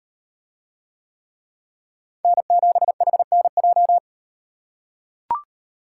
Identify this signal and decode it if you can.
Morse code, heard as a keyed tone, about 38 wpm, 700 Hz. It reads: N7HNJ